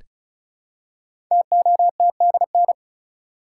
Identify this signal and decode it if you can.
Morse code, heard as a keyed tone, about 36 wpm, 700 Hz. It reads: TOTDN
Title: Code Groups - 5 Letters - 35wpm